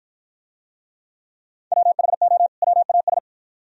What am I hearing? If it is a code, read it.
Morse code, heard as a keyed tone, about 53 wpm, 700 Hz. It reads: WHO WAS